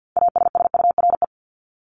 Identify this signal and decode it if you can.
Morse code, heard as a keyed tone, about 50 wpm, 700 Hz. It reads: ASSURE
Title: Top 1501-1750 Words 50wpm